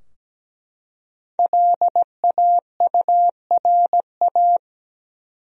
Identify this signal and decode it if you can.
Morse code, heard as a keyed tone, about 17 wpm, 700 Hz. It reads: LAURA
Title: USA Common Names - Max 5 Letters - 17wpm